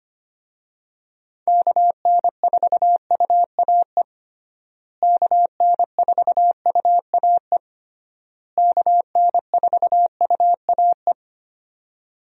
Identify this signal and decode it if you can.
Morse code, heard as a keyed tone, about 25 wpm, 700 Hz. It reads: KN4UAE KN4UAE KN4UAE
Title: USA Call Signs - 6 Characters - Repeated 3x - 25wpm